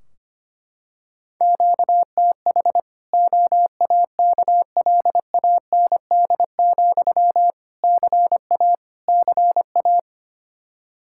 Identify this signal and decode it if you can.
Morse code, heard as a keyed tone, about 25 wpm, 700 Hz. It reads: QTH OAKLAND, CA CA
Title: Practice QSO - Exchange 1 - Beginning - 25wpm